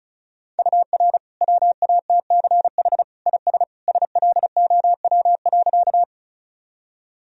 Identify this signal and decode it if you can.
Morse code, heard as a keyed tone, about 36 wpm, 700 Hz. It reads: UR WATCH IS SLOW.
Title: Sentences from Top 400 Words and Abbreviations 35wpm